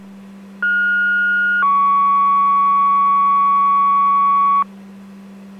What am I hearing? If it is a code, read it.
Two-tone page, heard as a one-second tone, then a three-second tone, about 1430 then 1120 Hz. It is heard from 0.6 to 4.6 s.